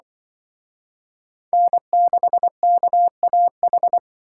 Morse code, heard as a keyed tone, about 25 wpm, 700 Hz. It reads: N6KAH
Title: Speed Racing - USA 1x3 Call Signs 20wpm